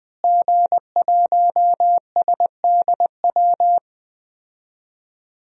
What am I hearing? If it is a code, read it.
Morse code, heard as a keyed tone, about 20 wpm, 700 Hz. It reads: G1SDW
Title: DXCC-100 Call Signs - 5 Characters - Repeated 3x - 20wpm